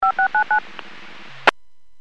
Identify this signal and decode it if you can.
DTMF, heard as telephone touch tones, tones of about 100 ms. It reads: 5699